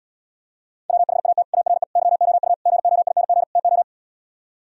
Morse code, heard as a keyed tone, about 77 wpm, 700 Hz. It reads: WHAT ARE YOU GOING TO